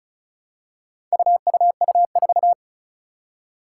Morse code, heard as a keyed tone, about 36 wpm, 700 Hz. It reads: UUU4
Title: Four Challenging Letter-Number Combinations 35wpm